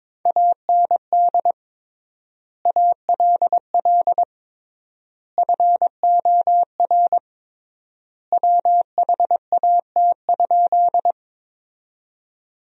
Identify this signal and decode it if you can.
Morse code, heard as a keyed tone, about 22 wpm, 700 Hz. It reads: AND ALL FOR WHAT?
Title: 3x Word-Spacing - Sentences from Top 100 Words - 22wpm